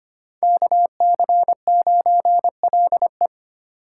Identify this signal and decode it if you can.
Morse code, heard as a keyed tone, about 25 wpm, 700 Hz. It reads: KC9LE
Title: USA Call Signs - 8x Character Spacing - 25wpm